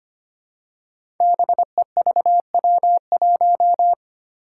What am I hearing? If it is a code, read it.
Morse code, heard as a keyed tone, about 25 wpm, 700 Hz. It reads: BEVW1
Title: Five Challenging Letter-Number Combinations 25wpm